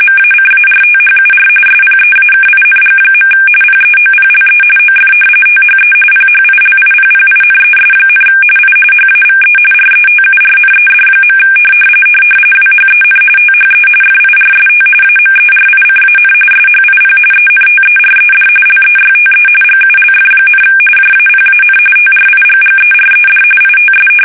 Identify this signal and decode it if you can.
RTTY, about 45.45 baud. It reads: MACARTHUR SPEAKING. PEOPLE OF THE PHILIPPINES, I HAVE RETURNED. WHEN THE FIRST AMERICAN SHIP ENTERED LEYTE GULF, JAPAN TRIGGERED ITS
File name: fsk850 / battleshipnj